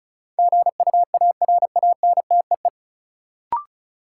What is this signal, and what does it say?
Morse code, heard as a keyed tone, about 36 wpm, 700 Hz. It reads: GUARANTEE
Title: Sentences from Top 1600 Words - New Word Drill - 35wpm